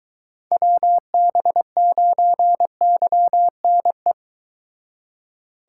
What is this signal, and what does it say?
Morse code, heard as a keyed tone, about 23 wpm, 700 Hz. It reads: WB9YNE